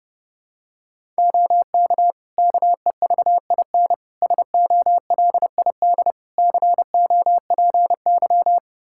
Morse code, heard as a keyed tone, about 30 wpm, 700 Hz. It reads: OK KEVIN SOLID COPY
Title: Practice QSO - Exchange 3 - More Info - 30wpm